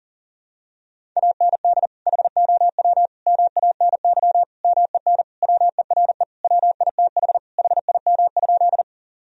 Morse code, heard as a keyed tone, about 40 wpm, 700 Hz. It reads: AND HOW MANY MEN WERE WITH HIM?
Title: Sentences from Top 1000 Words - Review Part 1 - 40wpm